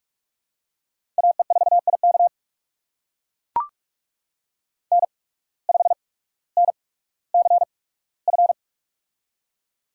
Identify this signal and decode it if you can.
Morse code, heard as a keyed tone, about 44 wpm, spaced as for 19 wpm, 690 Hz. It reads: AE4IK N5NCF
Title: USA Call Signs - 5 Characters - 8x Character Spacing - 45wpm